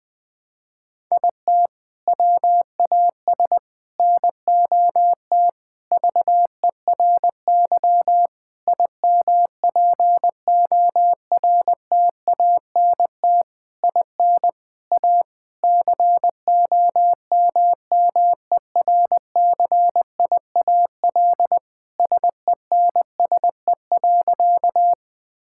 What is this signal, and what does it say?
Morse code, heard as a keyed tone, about 20 wpm, 700 Hz. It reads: IT WAS NOT VERY IMPORTANT IN A COMMERCIAL SENSE.
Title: Sentences from Top 1000 Words - Review Part 5 - 20wpm